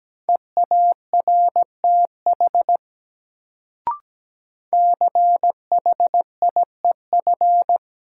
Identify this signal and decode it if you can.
Morse code, heard as a keyed tone, about 17 wpm, 700 Hz. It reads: EARTH CHIEF
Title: Sets of 5 Words Encore - New Word Drill - 17wpm